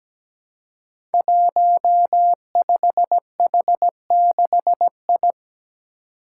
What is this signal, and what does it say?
Morse code, heard as a keyed tone, about 17 wpm, 700 Hz. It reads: 15H6I
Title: Five Challenging Letter-Number Combinations 17wpm